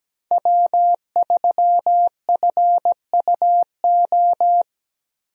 Morse code, heard as a keyed tone, about 17 wpm, 700 Hz. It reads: W3FUO
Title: USA Call Signs - 5 Characters - Repeated 3x - 17wpm